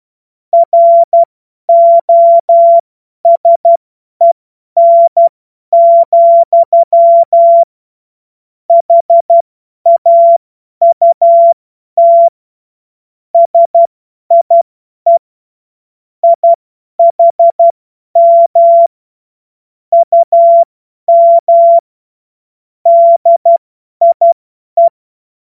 Morse code, heard as a keyed tone, about 12 wpm, spaced as for 10 wpm, 680 Hz. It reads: ROSEN, HAUT SIE IHM UM DIE
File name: Morsekurs / Witze / Witz 07. Juni 2022 50 F70